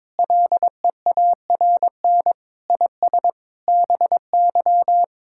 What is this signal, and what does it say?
Morse code, heard as a keyed tone, about 22 wpm, 700 Hz. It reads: LEARN IS BY
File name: practice / 22wpm / Sentences with 4-Word Phrases - 22wpm